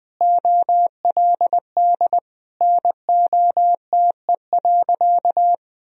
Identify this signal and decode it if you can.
Morse code, heard as a keyed tone, about 20 wpm, 700 Hz. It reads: OLD NOTE.
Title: Sentences from Top 1000 Words - Review Part 2 - 20wpm